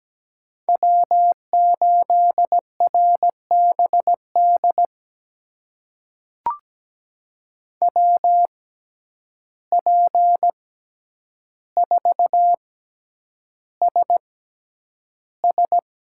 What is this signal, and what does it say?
Morse code, heard as a keyed tone, about 17 wpm, spaced as for 9 wpm, 700 Hz. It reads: W8RBD WP4SS
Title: USA Call Signs - 5 Characters - 6x Character Spacing - 17wpm